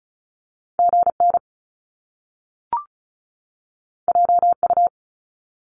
Morse code, heard as a keyed tone, about 36 wpm, 700 Hz. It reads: GN JU
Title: Top Two Letter Combinations in Words 35wpm